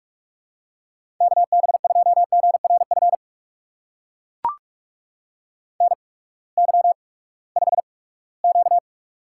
Morse code, heard as a keyed tone, about 44 wpm, spaced as for 22 wpm, 700 Hz. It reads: KB2GRF NY5Q
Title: USA Call Signs - 8x Character Spacing - 45wpm